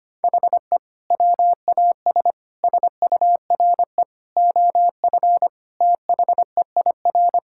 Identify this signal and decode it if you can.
Morse code, heard as a keyed tone, about 25 wpm, 700 Hz. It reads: HE WAS SURE OF THEIR
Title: Sentences from Top 1000 Words - Review Part 6 - 25wpm